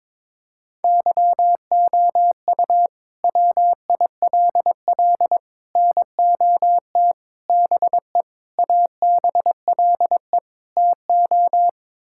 Morse code, heard as a keyed tone, about 22 wpm, 700 Hz. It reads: YOU WILL NOT BE ABLE TO
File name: practice / 22wpm / Sets of 6 Words - 22wpm